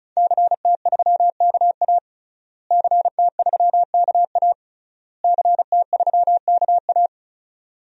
Morse code, heard as a keyed tone, about 34 wpm, 700 Hz. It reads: CT3KA CT3KA CT3KA
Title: DXCC-100 Call Signs - 5 Characters - Repeated 3x - 35wpm